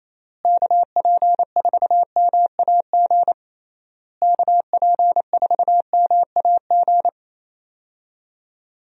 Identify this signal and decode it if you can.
Morse code, heard as a keyed tone, about 28 wpm, 700 Hz. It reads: KP4MAG KP4MAG
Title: USA Call Signs - 6 Characters - Repeated 3x - 28wpm